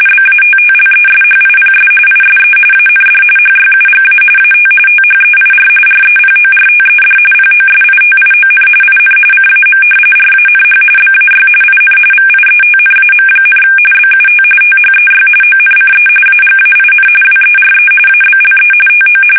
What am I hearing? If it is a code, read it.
RTTY, about 45.45 baud. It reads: BATTLESHIPS ISE AND HYUGA AT THEIR REPORTED POSITION IN CAM RANH BAY IN VIETNAM, BUT THEY ONCE AGAIN WERE ABLE TO
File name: fsk850 / battleshipnj